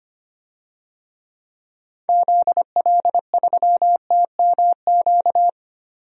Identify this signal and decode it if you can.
Morse code, heard as a keyed tone, about 25 wpm, 700 Hz. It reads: ZL3TMQ
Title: DXCC-100 Call Signs - 6 Characters - 25wpm